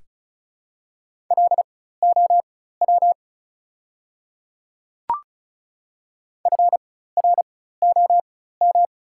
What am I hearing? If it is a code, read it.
Morse code, heard as a keyed tone, about 36 wpm, spaced as for 16 wpm, 690 Hz. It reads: LOW FROM
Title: 4x Character Spacing - Top 300 Words - 35wpm